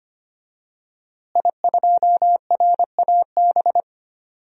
Morse code, heard as a keyed tone, about 25 wpm, 700 Hz. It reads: I2RAB
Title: DXCC-100 Call Signs - 5 Characters - Repeated 3x - 25wpm